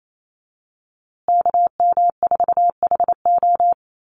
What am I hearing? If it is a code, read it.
Morse code, heard as a keyed tone, about 28 wpm, 700 Hz. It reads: KM4HO
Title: USA Call Signs Repeated 3x 28wpm